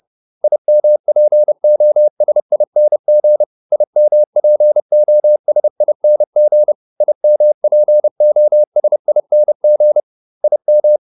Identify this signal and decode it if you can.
Morse code, heard as a keyed tone, about 30 wpm, 600 Hz. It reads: IMPOSING IMPOSING IMPOSING IM
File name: IM8 30